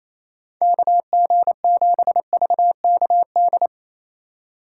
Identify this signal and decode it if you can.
Morse code, heard as a keyed tone, about 28 wpm, 700 Hz. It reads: KG7VKD
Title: USA Call Signs - 6 Characters - 2x Character Spacing - 28wpm